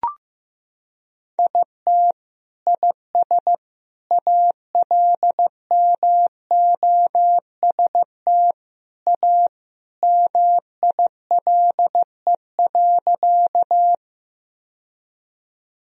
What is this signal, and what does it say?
Morse code, heard as a keyed tone, about 15 wpm, 700 Hz. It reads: IT IS ALMOST A MILE.
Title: Sentences from Top 1000 Words - Review Part 4 - 15wpm